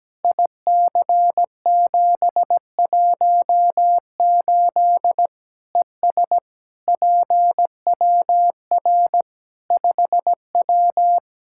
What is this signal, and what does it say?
Morse code, heard as a keyed tone, about 17 wpm, 700 Hz. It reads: IC718 ES PWR 5W
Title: Practice QSO - Exchange 2 - Info - 17wpm